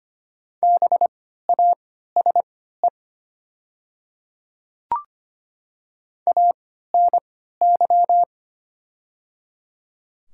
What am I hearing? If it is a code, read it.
Morse code, heard as a keyed tone, about 25 wpm, spaced as for 14 wpm, 700 Hz. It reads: BASE ANY
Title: 3x Character Spacing - Top 300 Words - 25wpm